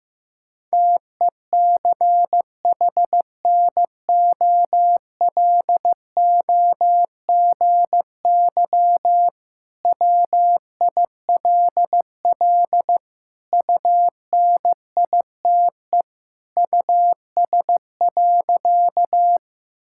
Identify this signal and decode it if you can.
Morse code, heard as a keyed tone, about 15 wpm, 700 Hz. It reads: TECHNOLOGY WILL UNITE US.
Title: Sentences from Top 1000 Words - Review Part 5 - 15wpm